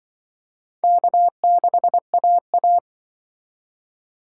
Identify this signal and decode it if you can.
Morse code, heard as a keyed tone, about 24 wpm, 700 Hz. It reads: K6AA
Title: Speed Racing - USA 1x2 Call Signs 20wpm